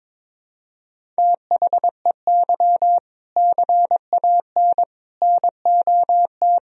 Morse code, heard as a keyed tone, about 22 wpm, 700 Hz. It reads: THEY CAN NOT
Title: Sentences from Top 1000 Words - Review Part 1 - 22wpm